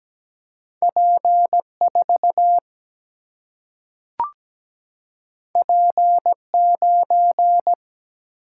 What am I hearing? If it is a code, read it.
Morse code, heard as a keyed tone, about 17 wpm, 700 Hz. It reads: P4 P9
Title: Single-Letter Single-Number 17wpm